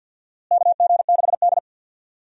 Morse code, heard as a keyed tone, about 47 wpm, 700 Hz. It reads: KG6D
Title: USA Call Signs - 4 Characters - Repeated 2x - 50wpm